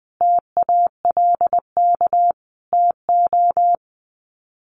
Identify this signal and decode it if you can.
Morse code, heard as a keyed tone, about 20 wpm, 700 Hz. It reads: TALK TO
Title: Sets of 5 Words Encore 20wpm